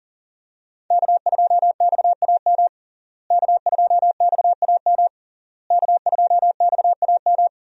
Morse code, heard as a keyed tone, about 37 wpm, 690 Hz. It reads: K2XAM K2XAM K2XAM
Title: USA Call Signs - 5 Characters - Repeated 3x - 40wpm